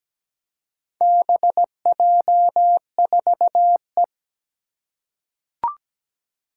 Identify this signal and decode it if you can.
Morse code, heard as a keyed tone, about 17 wpm, 700 Hz. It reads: BJ4E
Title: Four Challenging Letter-Number Combinations 17wpm